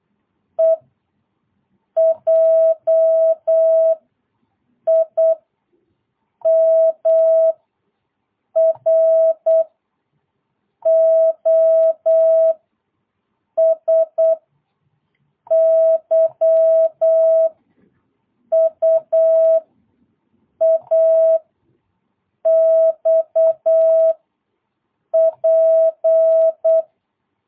Morse code, heard as a keyed tone, about 8 wpm, 650 Hz. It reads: E J I M R O S Y U A X P
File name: jh8cbh / 2022cw